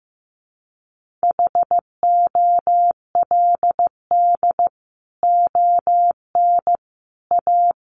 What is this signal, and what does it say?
Morse code, heard as a keyed tone, about 15 wpm, 700 Hz. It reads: HOLD ON A
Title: Sentences from Top 500 Words 15wpm